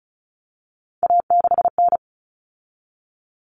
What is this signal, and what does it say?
Morse code, heard as a keyed tone, about 34 wpm, 700 Hz. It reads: A6N
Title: All Cut Numbers - 3-Digit 35wpm